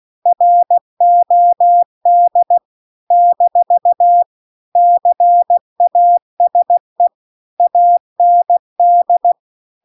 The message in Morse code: ROD - CASE AND

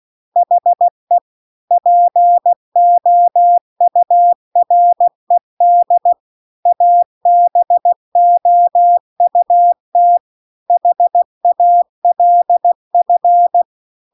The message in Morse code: HE POURED ABOUT HALF

